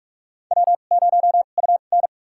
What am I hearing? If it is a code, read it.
Morse code, heard as a keyed tone, about 46 wpm, spaced as for 33 wpm, 700 Hz. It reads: W0UN
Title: USA Call Signs - 4 Characters - 2x Character Spacing - 45wpm